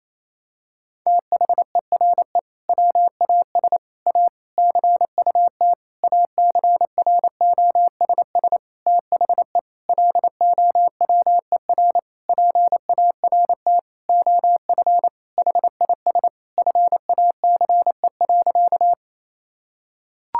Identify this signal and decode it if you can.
Morse code, heard as a keyed tone, about 28 wpm, 700 Hz. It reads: THERE WAS A CUT ACROSS THE LOWER PART OF HIS FACE.
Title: Sentences from Top 1000 Words - Review Part 5 - 28wpm